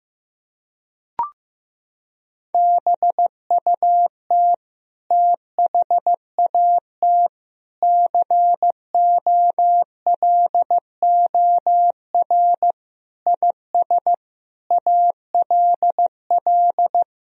Morse code, heard as a keyed tone, about 15 wpm, 700 Hz. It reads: BUT THAT COLOR IS ALL
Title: Sentences from Top 1000 Words - Review Part 2 - 15wpm